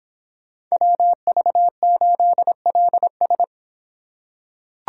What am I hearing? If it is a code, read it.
Morse code, heard as a keyed tone, about 26 wpm, 700 Hz. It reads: WV8LS